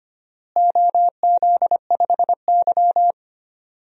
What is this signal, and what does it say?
Morse code, heard as a keyed tone, about 25 wpm, 700 Hz. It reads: OZ5Y
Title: DXCC-100 Call Signs - 4 Characters - Repeated 2x - 25wpm